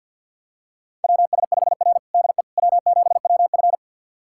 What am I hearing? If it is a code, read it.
Morse code, heard as a keyed tone, about 51 wpm, 700 Hz. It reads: WS5R DE W7WF
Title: USA Callsign DE Callsign - 4 Characters - 50wpm